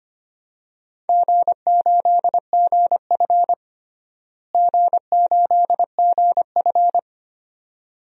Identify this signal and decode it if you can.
Morse code, heard as a keyed tone, about 25 wpm, 700 Hz. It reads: G8GF G8GF
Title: DXCC-100 Call Signs - 4 Characters - Repeated 2x - 25wpm